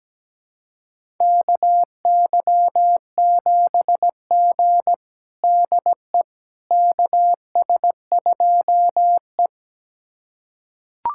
Morse code, heard as a keyed tone, about 17 wpm, 700 Hz. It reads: KY7G DE KS2E
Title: USA Callsign DE Callsign - 4 Characters - 17wpm